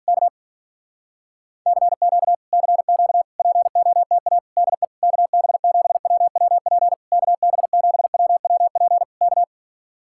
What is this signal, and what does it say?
Morse code, heard as a keyed tone, about 44 wpm, 700 Hz. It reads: K CQ CQ POTA DE KB7WWP KB7WWP K
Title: Speed-Racing Common QSO Phrases Encore - 35wpm